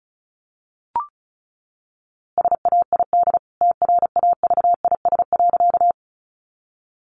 Morse code, heard as a keyed tone, about 34 wpm, 700 Hz. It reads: SAID TRAVIS.
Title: A Sound Of Thunder 35wpm